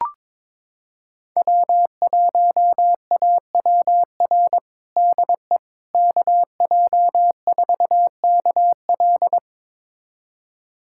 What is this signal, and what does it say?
Morse code, heard as a keyed tone, about 22 wpm, 700 Hz. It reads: W1AWR DE KJ4KL
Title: USA Callsign DE Callsign - 5 Characters - 22wpm